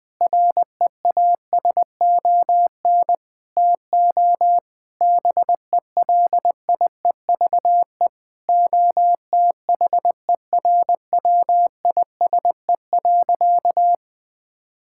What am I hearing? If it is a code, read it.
Morse code, heard as a keyed tone, about 21 wpm, 700 Hz. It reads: REASON TO BELIEVE OTHERWISE.
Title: Sentences with 6-Word Phrases - Spoken Beginning - 20wpm